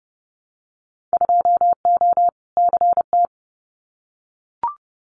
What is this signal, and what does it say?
Morse code, heard as a keyed tone, about 30 wpm, 700 Hz. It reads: 2O CT
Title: Winter Field Day Exchanges 30wpm